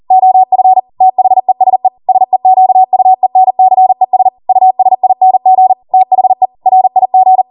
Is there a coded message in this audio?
Yes: Morse code from 0.1 to 7.4 s, heard as a keyed tone, about 40 wpm, 750 Hz. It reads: OF THESE SEQUENCES USING THE RIG